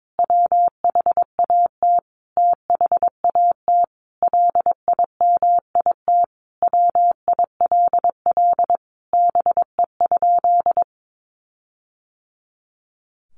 Morse code, heard as a keyed tone, about 22 wpm, 700 Hz. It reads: WHAT THAT LIMIT WILL BE?
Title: Sentences from Top 1500 Words 22wpm